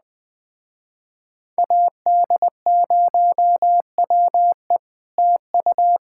Morse code, heard as a keyed tone, about 20 wpm, 700 Hz. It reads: AD0WE TU
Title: Speed-Racing Common QSO Phrases - 17wpm